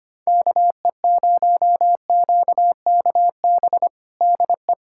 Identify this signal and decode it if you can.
Morse code, heard as a keyed tone, about 25 wpm, 700 Hz. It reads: KE0QKB DE
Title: USA Callsign DE Callsign - 6 Characters - 25wpm